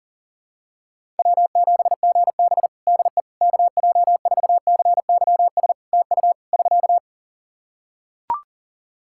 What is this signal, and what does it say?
Morse code, heard as a keyed tone, about 40 wpm, 700 Hz. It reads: W7GB DE KJ4CYS TU <SK>
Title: Practice QSO - Exchange 4 - Ending - 40wpm